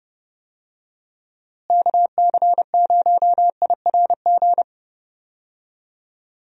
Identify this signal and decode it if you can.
Morse code, heard as a keyed tone, about 30 wpm, 700 Hz. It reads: KC0IRG
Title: USA Call Signs - 6 Characters - Repeated 2x - 30wpm